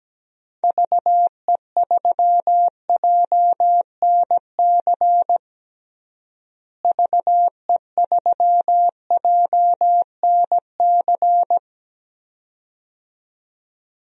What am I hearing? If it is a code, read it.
Morse code, heard as a keyed tone, about 17 wpm, 700 Hz. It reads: VE3JNC VE3JNC
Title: DXCC-100 Call Signs - 6 Characters - Repeated 2x - 17wpm